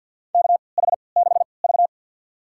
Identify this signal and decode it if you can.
Morse code, heard as a keyed tone, about 52 wpm, spaced as for 29 wpm, 700 Hz. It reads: KH6V
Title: USA Call Signs - 3x Character Spacing - 50wpm